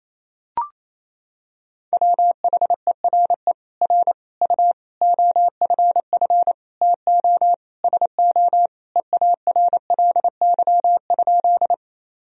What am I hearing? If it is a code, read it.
Morse code, heard as a keyed tone, about 28 wpm, 700 Hz. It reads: WHERE R U OFF TO SO EARLY?
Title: Sentences from Top 400 Words and Abbreviations 28wpm